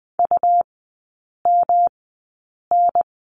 Morse code, heard as a keyed tone, about 20 wpm, spaced as for 8 wpm, 700 Hz. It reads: UMN
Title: 107 Lesson 18e - Letter m - Three Characters 20wpm